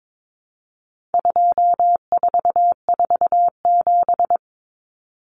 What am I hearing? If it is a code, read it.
Morse code, heard as a keyed tone, about 22 wpm, 700 Hz. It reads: 2447
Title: Numbers - 4-Digit 22wpm